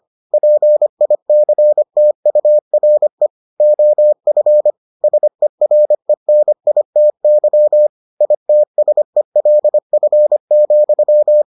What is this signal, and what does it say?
Morse code, heard as a keyed tone, about 25 wpm, 600 Hz. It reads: PICTURE OF SERENITY ITSELF,